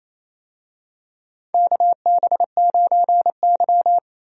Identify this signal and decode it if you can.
Morse code, heard as a keyed tone, about 28 wpm, 700 Hz. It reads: KB9Y
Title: USA Call Signs - 4 Characters - 28wpm